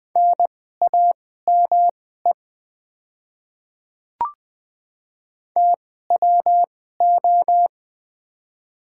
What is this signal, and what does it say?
Morse code, heard as a keyed tone, about 20 wpm, spaced as for 15 wpm, 700 Hz. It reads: NAME TWO
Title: 2x Character Spacing - Top 200 Words - 20wpm